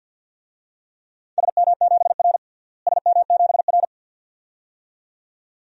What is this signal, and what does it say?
Morse code, heard as a keyed tone, about 51 wpm, 700 Hz. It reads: SM7R SM7R
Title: DXCC-100 Call Signs - 4 Characters - Repeated 2x - 50wpm